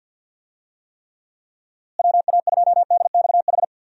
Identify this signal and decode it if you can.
Morse code, heard as a keyed tone, about 53 wpm, 700 Hz. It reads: WA2DXH
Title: USA Call Signs - 6 Characters - Repeated 2x - 50wpm